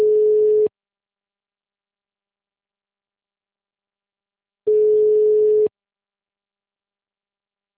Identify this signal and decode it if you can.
Ringback tone (Europe, 425 Hz).